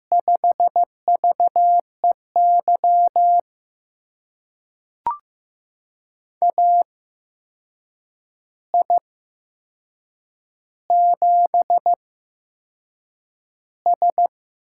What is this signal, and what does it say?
Morse code, heard as a keyed tone, about 15 wpm, spaced as for 6 wpm, 700 Hz. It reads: 5VEY AI7S
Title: USA Call Signs - 8x Character Spacing - 15wpm